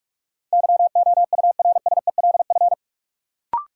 Morse code, heard as a keyed tone, about 45 wpm, 700 Hz. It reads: YOURSELF
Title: Sentences from Top 600 Words - New Word Drill - 45wpm